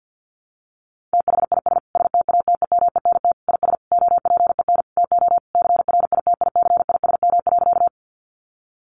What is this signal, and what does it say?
Morse code, heard as a keyed tone, about 50 wpm, 700 Hz. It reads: THIS STATEMENT IS OPEN TO CRITICISM.
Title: Sentences from Top 1300 Words 50wpm